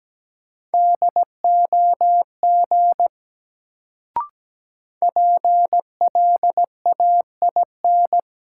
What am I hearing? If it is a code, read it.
Morse code, heard as a keyed tone, about 17 wpm, 700 Hz. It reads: DOG PLAIN
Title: Sentences from Top 400 Words - New Word Drill - 17wpm